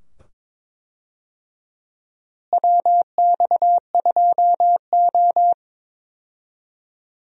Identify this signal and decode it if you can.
Morse code, heard as a keyed tone, about 22 wpm, 700 Hz. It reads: WX2O
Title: USA Call Signs - 4 Characters - Repeated 2x - 22wpm